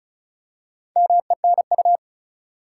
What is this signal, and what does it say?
Morse code, heard as a keyed tone, about 36 wpm, 700 Hz. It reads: MENU
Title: Four Letter Words - 35wpm